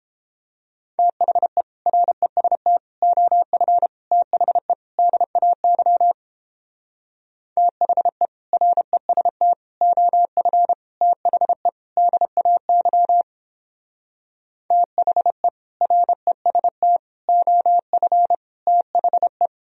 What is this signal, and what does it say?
Morse code, heard as a keyed tone, about 29 wpm, 700 Hz. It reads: THE REST OF THE DAY THE REST OF THE DAY THE REST OF THE